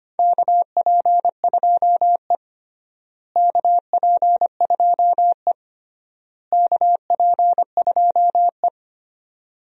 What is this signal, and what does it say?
Morse code, heard as a keyed tone, about 25 wpm, 700 Hz. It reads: KP2E KP2E KP2E
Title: DXCC-100 Call Signs - 4 Characters - Repeated 3x - 25wpm